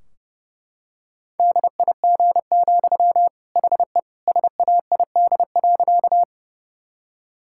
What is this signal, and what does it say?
Morse code, heard as a keyed tone, about 28 wpm, 700 Hz. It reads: DIG, HE SAID.
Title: Sentences from Top 1600 Words 30wpm